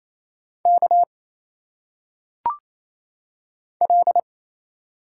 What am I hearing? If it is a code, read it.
Morse code, heard as a keyed tone, about 27 wpm, 700 Hz. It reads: K L
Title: Single Letters - Phonetic Pronunciation 28wpm